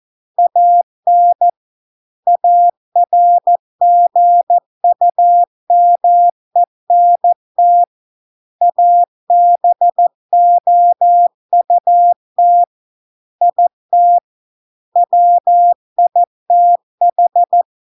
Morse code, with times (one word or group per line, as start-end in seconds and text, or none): 0.4-1.5 AN
2.3-7.8 ARGUMENT
8.6-12.6 ABOUT
13.4-14.2 IT
15.0-17.6 WITH